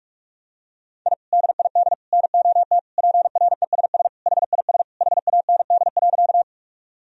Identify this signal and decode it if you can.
Morse code, heard as a keyed tone, about 46 wpm, 700 Hz. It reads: I DID NOT PRESS HIS HAND.
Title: Sentences from Top 1000 Words - Review Part 1 - 45wpm